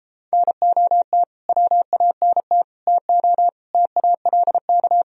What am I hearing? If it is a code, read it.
Morse code, heard as a keyed tone, about 33 wpm, 700 Hz. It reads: NOT WANT TO TALK